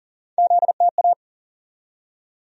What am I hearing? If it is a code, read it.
Morse code, heard as a keyed tone, about 40 wpm, 700 Hz. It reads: ZTA
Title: USA Call Sign Postfixes - 3 Letters 40wpm